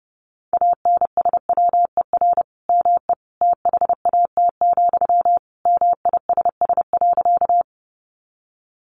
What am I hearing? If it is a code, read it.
Morse code, heard as a keyed tone, about 30 wpm, 700 Hz. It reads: ANSWER ME THAT, MISS.
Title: Sentences from Top 500 Words 30wpm